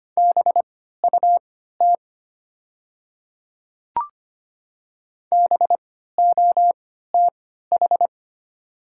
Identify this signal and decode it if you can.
Morse code, heard as a keyed tone, about 25 wpm, spaced as for 14 wpm, 700 Hz. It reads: BUT BOTH
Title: 3x Character Spacing - Top 300 Words - 25wpm